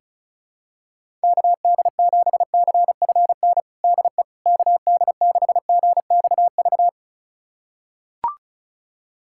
Morse code, heard as a keyed tone, about 36 wpm, 700 Hz. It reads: KD7CFN DE KD6GXV
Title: USA Callsign DE Callsign 35wpm